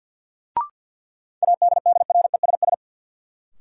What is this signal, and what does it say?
Morse code, heard as a keyed tone, about 52 wpm, 700 Hz. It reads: ADDRESS
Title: Sets of 5 Words Encore - New Word Drill - 50wpm